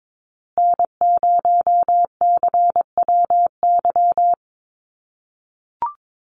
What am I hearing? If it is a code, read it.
Morse code, heard as a keyed tone, about 22 wpm, 700 Hz. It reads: N0CWY
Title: USA Call Signs Repeated 3x 22wpm